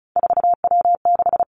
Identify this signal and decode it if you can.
Morse code, heard as a keyed tone, about 36 wpm, 700 Hz. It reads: 4W6
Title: Three Challenging Letter-Number Combinations 35wpm